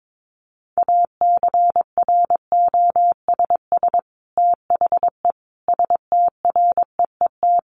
Morse code, heard as a keyed tone, about 22 wpm, 700 Hz. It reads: ACROSS THE STREET
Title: Sets of 5 Words Encore 22wpm